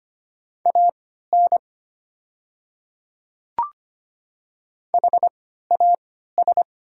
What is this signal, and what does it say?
Morse code, heard as a keyed tone, about 25 wpm, spaced as for 14 wpm, 700 Hz. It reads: AN HAS
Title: 3x Character Spacing - Top 200 Words - 25wpm